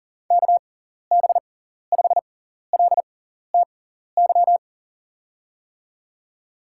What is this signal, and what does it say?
Morse code, heard as a keyed tone, about 40 wpm, spaced as for 14 wpm, 700 Hz. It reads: KB5LTY
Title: USA Call Signs - 6 Characters - 6x Character Spacing - 40wpm